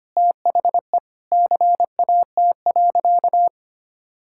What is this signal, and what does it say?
Morse code, heard as a keyed tone, about 25 wpm, 700 Hz. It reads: THE CAT.
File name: practice / 25wpm / Sentences with 3-Word Phrases - 25wpm